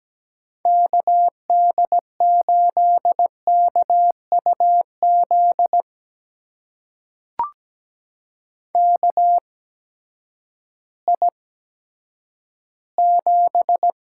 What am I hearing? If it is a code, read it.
Morse code, heard as a keyed tone, about 17 wpm, spaced as for 10 wpm, 700 Hz. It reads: KD8KUZ KI7
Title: USA Call Signs - 6 Characters - 8x Character Spacing - 17wpm